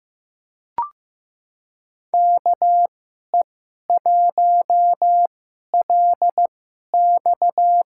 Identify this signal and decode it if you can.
Morse code, heard as a keyed tone, about 15 wpm, spaced as for 11 wpm, 700 Hz. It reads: KE1LX
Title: USA Call Signs - 5 Characters - 2x Character Spacing - 15wpm